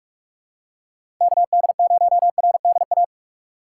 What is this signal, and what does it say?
Morse code, heard as a keyed tone, about 41 wpm, 700 Hz. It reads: KD0RDA
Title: USA Call Signs - 6 Characters - 2x Character Spacing - 45wpm